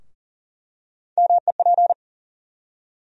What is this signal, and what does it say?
Morse code, heard as a keyed tone, about 40 wpm, 700 Hz. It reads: MEP
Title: USA Call Sign Postfixes - 3 Letters 40wpm